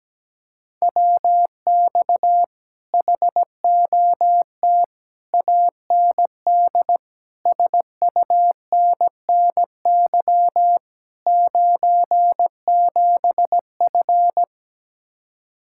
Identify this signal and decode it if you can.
Morse code, heard as a keyed tone, about 17 wpm, 700 Hz. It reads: WX HOT AND SUNNY 97F
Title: Speed-Racing Common QSO Phrases - 17wpm